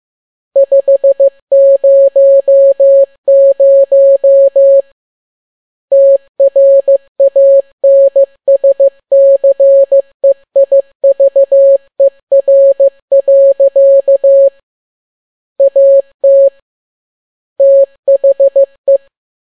Morse code, heard as a keyed tone, about 15 wpm, 550 Hz. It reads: 500 TRANSCEIVER. AT THE
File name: n6tt p1 15C13